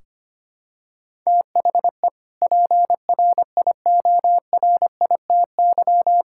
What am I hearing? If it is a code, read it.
Morse code, heard as a keyed tone, about 25 wpm, 700 Hz. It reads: THE PRIORITY
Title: Sentences from Top 1600 Words 25wpm